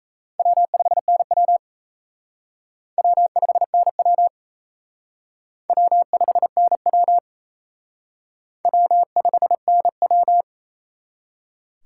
Morse code, heard as a keyed tone, about 34 wpm, 700 Hz. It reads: W5NW W5NW W5NW W5NW
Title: Speed Racing - USA Call Signs 28wpm